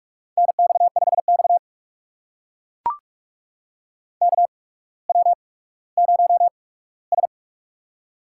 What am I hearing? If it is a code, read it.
Morse code, heard as a keyed tone, about 44 wpm, spaced as for 19 wpm, 700 Hz. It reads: NX5X KW0S
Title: USA Call Signs - 4 Characters - 8x Character Spacing - 45wpm